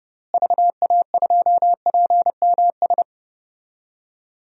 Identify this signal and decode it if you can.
Morse code, heard as a keyed tone, about 29 wpm, 700 Hz. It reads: VA2PMS
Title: DXCC-100 Call Signs - 6 Characters - Repeated 2x - 30wpm